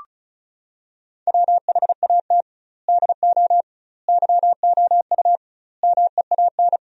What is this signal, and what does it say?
Morse code, heard as a keyed tone, about 36 wpm, 700 Hz. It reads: WHAT DO YOU MEAN
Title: 2x Word-Spacing - Sentences from Top 200 Words - 35wpm